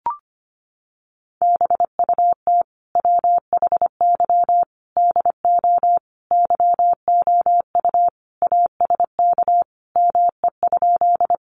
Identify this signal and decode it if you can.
Morse code, heard as a keyed tone, about 25 wpm, 700 Hz. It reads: BUT WHY DO YOU ASK ME?
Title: Sentences from Top 200 Words 25wpm